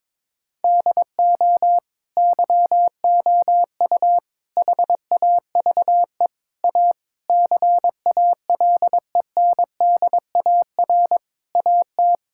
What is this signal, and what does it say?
Morse code, heard as a keyed tone, about 22 wpm, 700 Hz. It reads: DO YOU HAVE A CALENDAR AT
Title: Sentences from Top 1600 Words 22wpm